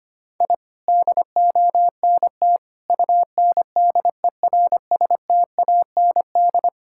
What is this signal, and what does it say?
Morse code, heard as a keyed tone, about 25 wpm, 700 Hz. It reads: I DONT UNDERSTAND
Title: Sentences from Top 1000 Words - Review Part 5 - 25wpm